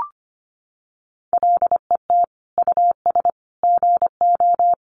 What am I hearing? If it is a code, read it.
Morse code, heard as a keyed tone, about 25 wpm, 700 Hz. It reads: LET US GO
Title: Sentences from Top 1100 Words 25wpm